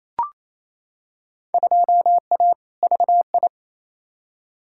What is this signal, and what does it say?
Morse code, heard as a keyed tone, about 28 wpm, 700 Hz. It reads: 2A VI
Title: Field Day Exchanges Encore - 28wpm